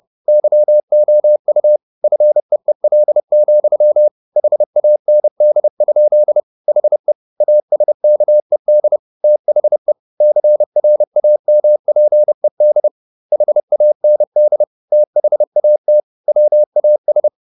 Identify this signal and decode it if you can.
Morse code, heard as a keyed tone, about 30 wpm, 600 Hz. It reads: YOU FEEL, HAND? HE ASKED THE CRAMPED HAND THAT WAS
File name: the old man and the sea / the old man and the sea 1 30wpm